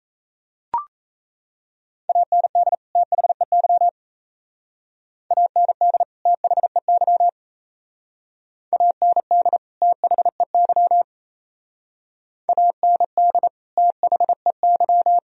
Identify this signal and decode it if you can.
Morse code, heard as a keyed tone, about 35 wpm, 700 Hz. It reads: AND THEY AND THEY AND THEY AND THEY